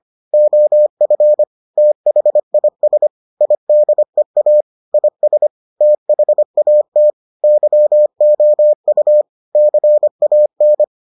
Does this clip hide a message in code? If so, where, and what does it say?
Yes: Morse code from 0.3 to 10.8 s, heard as a keyed tone, about 25 wpm, 600 Hz. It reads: OF THIS IDEA IS THAT YOU CAN